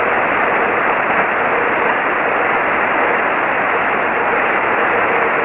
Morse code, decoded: none

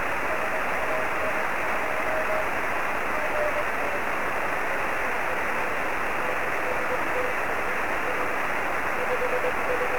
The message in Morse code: none

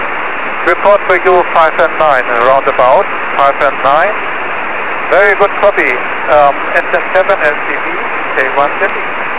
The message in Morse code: none

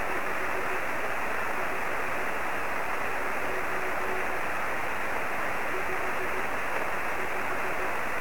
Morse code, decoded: none

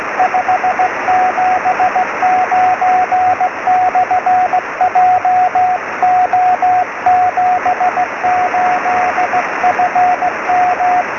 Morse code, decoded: 579/JO78FM

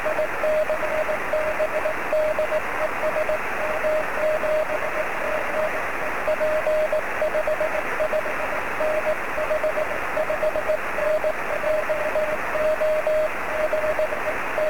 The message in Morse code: none

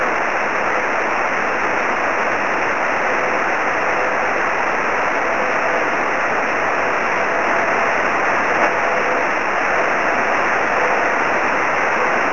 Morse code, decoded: none

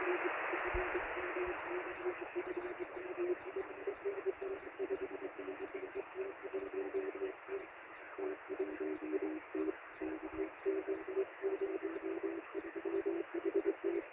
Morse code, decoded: none

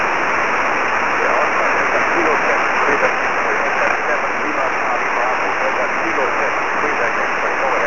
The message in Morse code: none

